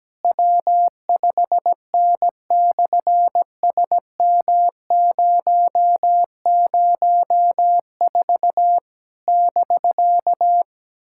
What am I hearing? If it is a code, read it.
Morse code, heard as a keyed tone, about 17 wpm, 700 Hz. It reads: W5N/SM004 <BK>
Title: SOTA QSO Practice 17wpm